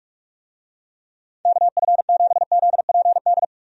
Morse code, heard as a keyed tone, about 47 wpm, 700 Hz. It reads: KF7ZPD
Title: USA Call Signs - 6 Characters - 45wpm